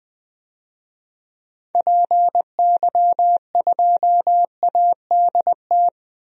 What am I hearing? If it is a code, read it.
Morse code, heard as a keyed tone, about 21 wpm, 700 Hz. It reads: PY2ADT
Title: DXCC-100 Call Signs - 6 Characters - Repeated 2x - 20wpm